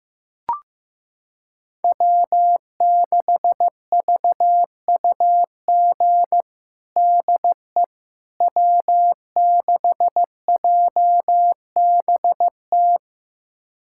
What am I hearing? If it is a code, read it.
Morse code, heard as a keyed tone, about 15 wpm, 700 Hz. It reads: W6VUG DE W6JBT
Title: USA Callsign DE Callsign - 5 Characters - 15wpm